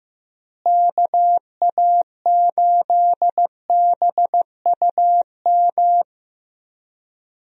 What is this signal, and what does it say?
Morse code, heard as a keyed tone, about 15 wpm, 700 Hz. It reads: KA8BUM
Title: USA Call Signs - 6 Characters - Repeated 3x - 15wpm